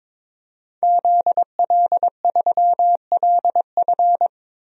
Morse code, heard as a keyed tone, about 22 wpm, 700 Hz. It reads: ZL3LF
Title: DXCC-100 Call Signs - 5 Characters - 22wpm